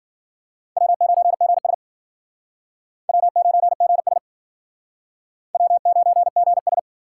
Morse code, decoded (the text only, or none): W9GS W9GS W9GS